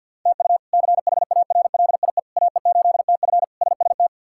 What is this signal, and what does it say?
Morse code, heard as a keyed tone, about 51 wpm, 700 Hz. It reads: TU CHARLIE AE8TF SST
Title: K1USN SST QSO Practice - 50wpm